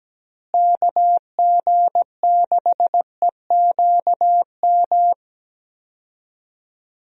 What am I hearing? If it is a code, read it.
Morse code, heard as a keyed tone, about 17 wpm, 700 Hz. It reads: KG6EQM
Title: USA Call Signs - 6 Characters - Repeated 3x - 17wpm